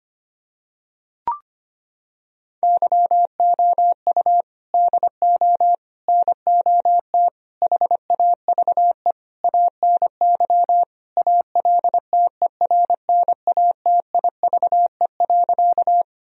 Morse code, heard as a keyed tone, about 25 wpm, 700 Hz. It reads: YOU DO NOT HAVE ANY ALTERNATIVE.
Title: Sentences from Top 1000 Words - Review Part 4 - 25wpm